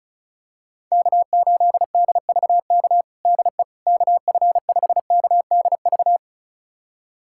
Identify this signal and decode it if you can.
Morse code, heard as a keyed tone, about 34 wpm, 700 Hz. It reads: K8DVK DE KF5KDV
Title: USA Callsign DE Callsign 35wpm